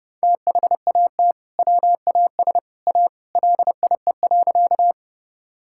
Morse code, heard as a keyed tone, about 32 wpm, 700 Hz. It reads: THAT WAS A LIE.
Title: Sentences from Top 1600 Words 30wpm